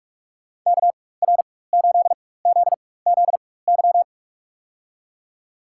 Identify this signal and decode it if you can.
Morse code, heard as a keyed tone, about 46 wpm, spaced as for 21 wpm, 700 Hz. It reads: KR8ZZY
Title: USA Call Signs - 4x Character Spacing - 45wpm